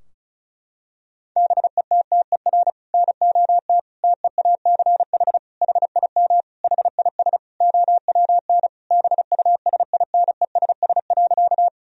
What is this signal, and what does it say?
Morse code, heard as a keyed tone, about 36 wpm, 700 Hz. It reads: BETTER NOT TEACH HIM HIS OWN BUSINESS.
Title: Sentences from Top 1000 Words - Review Part 2 - 35wpm